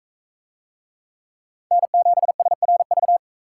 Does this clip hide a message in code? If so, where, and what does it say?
Yes: Morse code from 1.7 to 3.2 s, heard as a keyed tone, about 40 wpm, 700 Hz. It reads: N7SRV